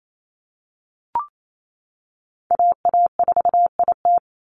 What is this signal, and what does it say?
Morse code, heard as a keyed tone, about 28 wpm, 700 Hz. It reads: AA4IT
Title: NA SOTA Chaser Callsigns 28wpm